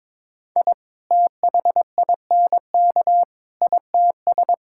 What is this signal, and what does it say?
Morse code, heard as a keyed tone, about 22 wpm, 700 Hz. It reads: I THINK ITS
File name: practice / 22wpm / Sets of 6 Words - 22wpm